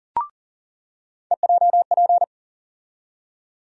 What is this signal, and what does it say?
Morse code, heard as a keyed tone, about 40 wpm, 700 Hz. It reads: EJP
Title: USA Call Sign Postfixes - 3 Letters 40wpm